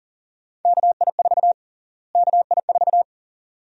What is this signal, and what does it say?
Morse code, heard as a keyed tone, about 40 wpm, 700 Hz. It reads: KI4 KI4
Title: USA Call Sign Prefixes - Repeated 2x - 40wpm